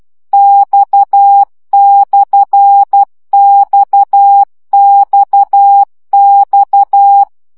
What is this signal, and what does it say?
Morse code, heard as a keyed tone, about 12 wpm, 800 Hz. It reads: X/XXX
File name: CW gyakorlas x per 10 baud